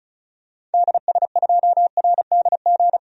Morse code, heard as a keyed tone, about 36 wpm, 700 Hz. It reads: DS2RDG
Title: DXCC-100 Call Signs - 6 Characters - 35wpm